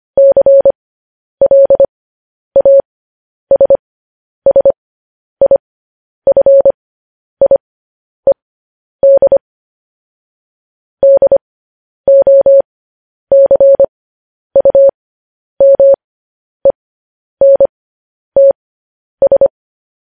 Morse code, heard as a keyed tone, about 25 wpm, spaced as for 10 wpm, 550 Hz. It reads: CLASSIFIED DOCUMENTS